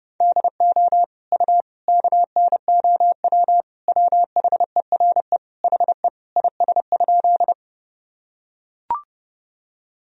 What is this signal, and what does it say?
Morse code, heard as a keyed tone, about 30 wpm, 700 Hz. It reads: DO U KNOW WHERE HE IS?
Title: Sentences from Top 300 Words and Abbreviations 30wpm